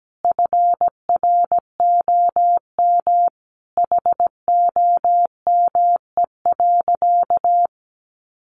Morse code, heard as a keyed tone, about 17 wpm, 700 Hz. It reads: FROM HOME.
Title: Sentences from Top 1500 Words 17wpm